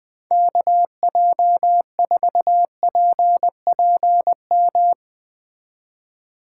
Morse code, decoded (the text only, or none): KJ4PPM